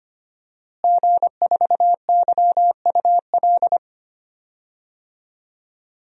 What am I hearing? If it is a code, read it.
Morse code, heard as a keyed tone, about 25 wpm, 700 Hz. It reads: G4YUL
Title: DXCC-100 Call Signs - 5 Characters - Repeated 3x - 25wpm